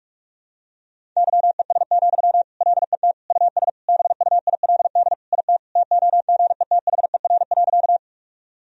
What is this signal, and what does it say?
Morse code, heard as a keyed tone, about 46 wpm, 700 Hz. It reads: YES, LET US BUILD IT TOGETHER.
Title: Sentences from Top 1000 Words - Review Part 1 - 45wpm